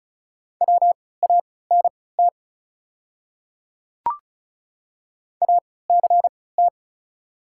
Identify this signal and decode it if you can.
Morse code, heard as a keyed tone, about 36 wpm, spaced as for 20 wpm, 700 Hz. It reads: WANT ACT
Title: 3x Character Spacing - Top 200 Words - 35wpm